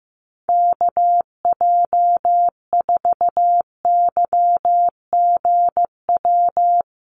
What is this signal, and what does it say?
Morse code, heard as a keyed tone, about 15 wpm, 700 Hz. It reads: KJ4YGW
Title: USA Call Signs Repeated 3x 15wpm